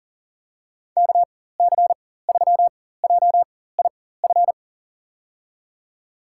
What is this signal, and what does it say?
Morse code, heard as a keyed tone, about 39 wpm, spaced as for 18 wpm, 700 Hz. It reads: KC3JIF
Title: USA Call Signs - 6 Characters - 4x Character Spacing - 40wpm